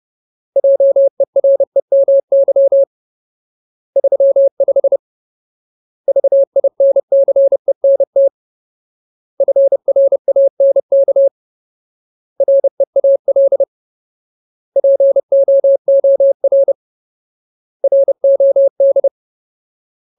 Morse code, heard as a keyed tone, about 30 wpm, 550 Hz. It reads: JEREMY 35 VINCENT FRANK REAL POOR ROD
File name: Ultimate Word List / 30WPM Ultimate